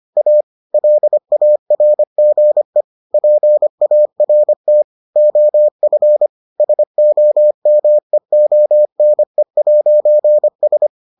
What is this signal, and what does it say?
Morse code, heard as a keyed tone, about 25 wpm, 600 Hz. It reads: A LARGE PART OF SOMEONE'S